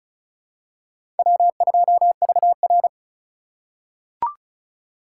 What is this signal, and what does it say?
Morse code, heard as a keyed tone, about 36 wpm, 700 Hz. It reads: W2VR
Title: USA Call Signs - 4 Characters - 2x Character Spacing - 35wpm